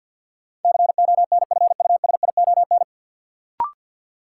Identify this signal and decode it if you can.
Morse code, heard as a keyed tone, about 52 wpm, 700 Hz. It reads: CONFUSION
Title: Sentences from Top 1600 Words - New Word Drill - 50wpm